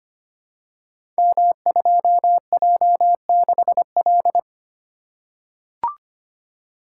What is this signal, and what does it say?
Morse code, heard as a keyed tone, about 25 wpm, 700 Hz. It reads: M2J6L
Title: Code Groups - 5 Letter:Numbers - 25wpm